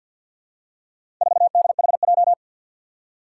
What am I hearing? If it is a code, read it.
Morse code, heard as a keyed tone, about 51 wpm, 700 Hz. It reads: 4DHJ
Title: Four Challenging Letter-Number Combinations 50wpm